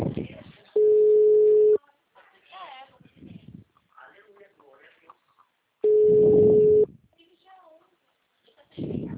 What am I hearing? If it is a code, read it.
Ringback tone (Europe, 425 Hz), with speech.
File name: out- -2013- - - .23302